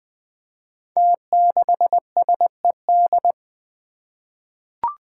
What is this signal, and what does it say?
Morse code, heard as a keyed tone, about 20 wpm, 700 Hz. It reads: T6SED
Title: Five Challenging Letter-Number Combinations 20wpm